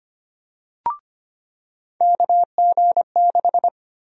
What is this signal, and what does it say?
Morse code, heard as a keyed tone, about 25 wpm, 700 Hz. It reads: KG6
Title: USA Call Sign Prefixes - 25wpm